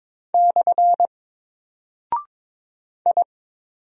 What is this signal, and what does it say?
Morse code, heard as a keyed tone, about 22 wpm, 700 Hz. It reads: / I
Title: Single Character (Letter:Number:Punctuation:Prosign) Alt 22wpm